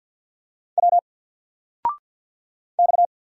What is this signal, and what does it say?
Morse code, heard as a keyed tone, about 50 wpm, 700 Hz. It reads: W X
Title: Single Letter-Number 50wpm